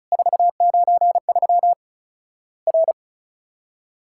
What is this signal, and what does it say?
Morse code, heard as a keyed tone, about 36 wpm, 700 Hz. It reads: 493 R
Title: Worked All Europe DX Contest - QTC Exchange - 35wpm